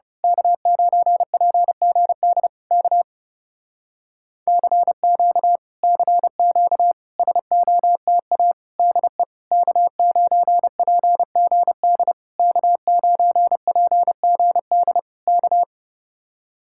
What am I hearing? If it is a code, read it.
Morse code, heard as a keyed tone, about 29 wpm, 700 Hz. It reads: K9PGD K CQ CQ SOTA DE K9PGD K9PGD K
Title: Speed-Racing Common QSO Phrases - 30wpm